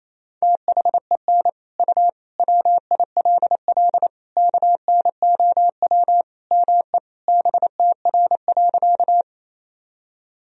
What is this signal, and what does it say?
Morse code, heard as a keyed tone, about 27 wpm, 700 Hz. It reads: THEN U WILL KNOW ME BTR.
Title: Sentences from Top 400 Words and Abbreviations 28wpm